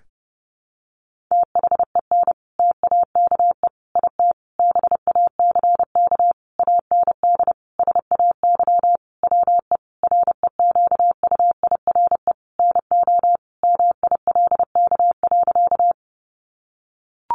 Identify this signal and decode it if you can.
Morse code, heard as a keyed tone, about 30 wpm, 700 Hz. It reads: THEN TAKE IT BACK AND SAY WE REQUIRE NO MILK.
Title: Sentences from Top 1500 Words 30wpm